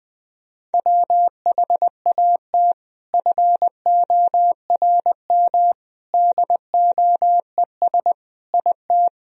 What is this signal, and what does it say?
Morse code, heard as a keyed tone, about 21 wpm, 700 Hz. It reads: WHAT FORM DOES IT
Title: Sentences from Top 1000 Words - Review Part 1 - 20wpm